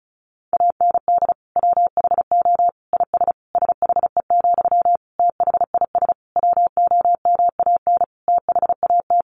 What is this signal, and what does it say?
Morse code, heard as a keyed tone, about 34 wpm, 700 Hz. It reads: AND WHO IS SHE, THIS WOMAN THAT
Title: Sentences from Top 1100 Words 35wpm